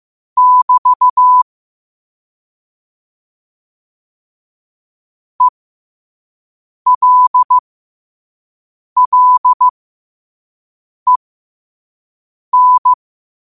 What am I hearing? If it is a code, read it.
Morse code, heard as a keyed tone, about 15 wpm, spaced as for 5 wpm, 1000 Hz. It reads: <BT> ELLEN
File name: qotdfiles / 5wpm